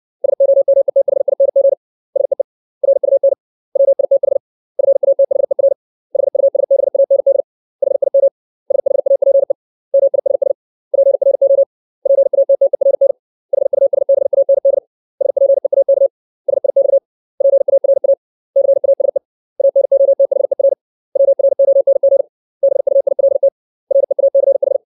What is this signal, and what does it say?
Morse code, heard as a keyed tone, about 60 wpm, 550 Hz. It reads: SOMETHING SEE RUN WITH FATHER HUSBAND HIM SHAPE MISS PRO PATTERN HUSBAND SPEAK SIX PARA CASE ANOTHER WRONG BUILT REACH
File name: 632 words / 632 Words 60WPM